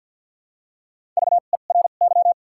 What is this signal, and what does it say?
Morse code, heard as a keyed tone, about 53 wpm, spaced as for 37 wpm, 700 Hz. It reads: VERY
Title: 2x Character Spacing - Top 500 Words - 50wpm